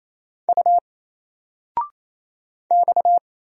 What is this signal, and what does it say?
Morse code, heard as a keyed tone, about 27 wpm, 700 Hz. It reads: U X
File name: Single-Letter-Number-28wpm